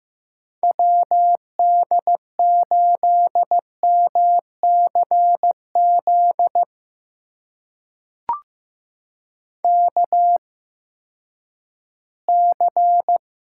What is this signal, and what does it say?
Morse code, heard as a keyed tone, about 16 wpm, spaced as for 11 wpm, 700 Hz. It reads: WD8MCZ KC
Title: USA Call Signs - 6 Characters - 8x Character Spacing - 15wpm